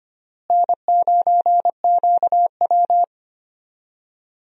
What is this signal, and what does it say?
Morse code, heard as a keyed tone, about 25 wpm, 700 Hz. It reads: N9QW
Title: USA Call Signs - 4 Characters - Repeated 2x - 25wpm